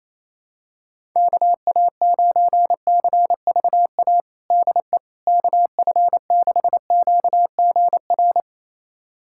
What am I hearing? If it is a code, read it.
Morse code, heard as a keyed tone, about 28 wpm, 700 Hz. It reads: KA9CVA DE KF6QGR
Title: USA Callsign DE Callsign 28wpm